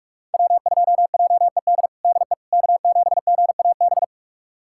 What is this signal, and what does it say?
Morse code, heard as a keyed tone, about 47 wpm, 700 Hz. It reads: W2JED DE K7GAB
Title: USA Callsign DE Callsign - 5 Characters - 45wpm